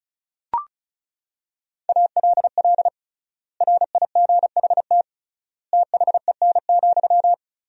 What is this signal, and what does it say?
Morse code, heard as a keyed tone, about 36 wpm, 700 Hz. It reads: ALL RIGHT THEN,
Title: 3x Word-Spacing - Sentences from Top 500 Words - 35wpm